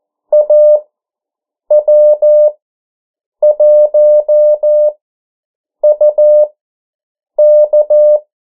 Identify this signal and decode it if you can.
Morse code, heard as a keyed tone, about 14 wpm, spaced as for 7 wpm, 600 Hz. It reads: AW1UK
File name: morse mp3 / 09 urok 8